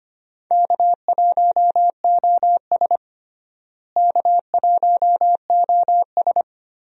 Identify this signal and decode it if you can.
Morse code, heard as a keyed tone, about 25 wpm, 700 Hz. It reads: K1OS K1OS
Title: USA Call Signs - 4 Characters - Repeated 3x - 25wpm